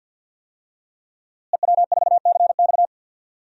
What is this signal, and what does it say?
Morse code, heard as a keyed tone, about 51 wpm, 700 Hz. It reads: EW4CX
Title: DXCC-100 Call Signs - 5 Characters - Repeated 2x - 50wpm